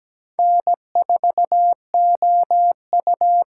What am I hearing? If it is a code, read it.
Morse code, heard as a keyed tone, about 17 wpm, 700 Hz. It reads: N4OU
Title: USA Call Signs - 5 Characters - 8x Character Spacing - 17wpm